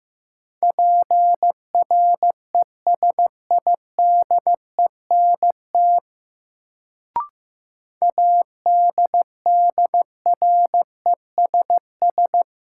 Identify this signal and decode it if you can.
Morse code, heard as a keyed tone, about 15 wpm, 700 Hz. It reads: PRESIDENT ADDRESS
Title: Sentences from Top 1000 Words - New Word Drill - 15wpm